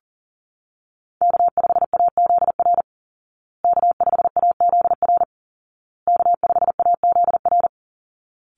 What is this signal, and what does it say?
Morse code, heard as a keyed tone, about 40 wpm, 700 Hz. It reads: K5AZR K5AZR K5AZR
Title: Call Signs Repeated 2x - 40wpm